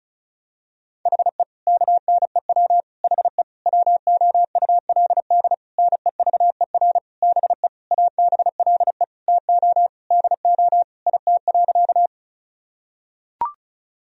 Morse code, heard as a keyed tone, about 34 wpm, 700 Hz. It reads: HE KNEW HE WOULD NEVER BE ABLE TO DO IT.
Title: Sentences from Top 1000 Words - Review Part 2 - 35wpm